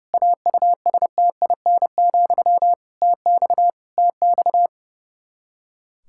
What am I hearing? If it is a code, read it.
Morse code, heard as a keyed tone, about 30 wpm, 700 Hz. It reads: AUSTIN, TX TX
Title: Practice QSO - Exchange 1 - Beginning - 30wpm